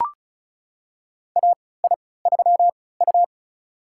Morse code, heard as a keyed tone, about 36 wpm, spaced as for 20 wpm, 700 Hz. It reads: AI3U
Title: USA Call Signs - 4 Characters - 3x Character Spacing - 35wpm